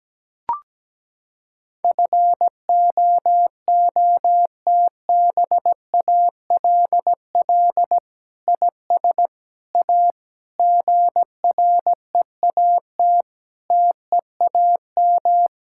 Morse code, heard as a keyed tone, about 17 wpm, 700 Hz. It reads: FOOTBALL IS A GREAT TEAM
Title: Sentences from Top 1600 Words 17wpm